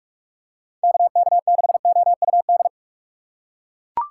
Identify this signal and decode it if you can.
Morse code, heard as a keyed tone, about 41 wpm, 700 Hz. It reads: KK6OUD
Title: USA Call Signs - 6 Characters - 2x Character Spacing - 45wpm